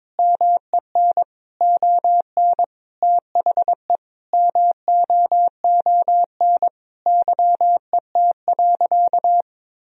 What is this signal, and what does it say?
Morse code, heard as a keyed tone, about 22 wpm, 700 Hz. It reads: MEN ON THE MOON YET.
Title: Sentences from Top 1000 Words - Review Part 3 - 22wpm